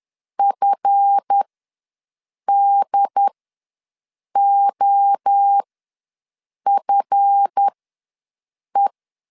Morse code, heard as a keyed tone, about 10 wpm, spaced as for 6 wpm, 800 Hz. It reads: FDOFE
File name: LEZIONI / 7 GF9Z / 7 GF ws 10wpm 3918